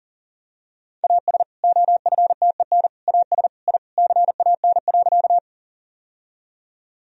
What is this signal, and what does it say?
Morse code, heard as a keyed tone, about 40 wpm, 700 Hz. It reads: AS OFTEN AS I CAN.
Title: Sentences from Top 1000 Words - Review Part 1 - 40wpm